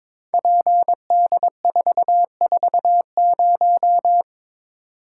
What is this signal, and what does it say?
Morse code, heard as a keyed tone, about 22 wpm, 700 Hz. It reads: PD440
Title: Code Groups - 5 Letter:Numbers - 22wpm